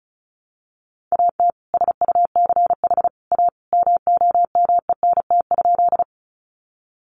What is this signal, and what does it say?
Morse code, heard as a keyed tone, about 34 wpm, 700 Hz. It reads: AT SUCH A MOMENT?
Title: Sentences from Top 1000 Words 35wpm